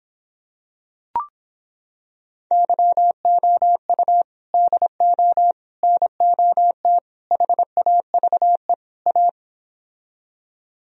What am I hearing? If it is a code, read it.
Morse code, heard as a keyed tone, about 26 wpm, 700 Hz. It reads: YOU DO NOT HAVE A